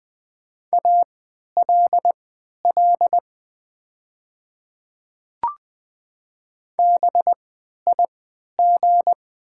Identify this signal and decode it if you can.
Morse code, heard as a keyed tone, about 20 wpm, spaced as for 11 wpm, 700 Hz. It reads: ALL BIG
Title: 3x Character Spacing - Top 200 Words - 20wpm